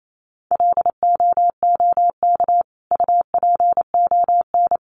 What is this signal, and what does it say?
Morse code, heard as a keyed tone, about 28 wpm, 700 Hz. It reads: LOOK UPON
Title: Sentences from Top 1100 Words 28wpm